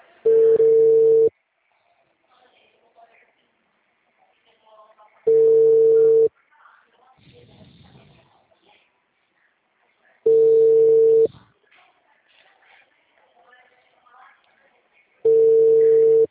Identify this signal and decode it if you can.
Ringback tone (China, 450 Hz), with speech.
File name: out- -2001- - - .17